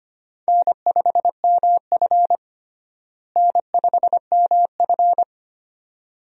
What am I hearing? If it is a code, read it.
Morse code, heard as a keyed tone, about 25 wpm, 700 Hz. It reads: N5MF N5MF
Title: USA Call Signs - 4 Characters - Repeated 2x - 25wpm